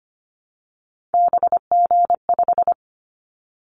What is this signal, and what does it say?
Morse code, heard as a keyed tone, about 25 wpm, 700 Hz. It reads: BG5
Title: Callsign Prefixes for Top 100 DXCC - 25wpm